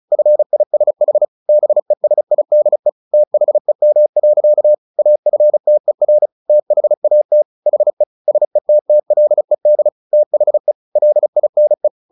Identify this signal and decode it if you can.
Morse code, heard as a keyed tone, about 34 wpm, 600 Hz. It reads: FISH BESIDE THEM. AFTER THAT HE SETTLED THE LINE
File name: the old man and the sea 2 35wpm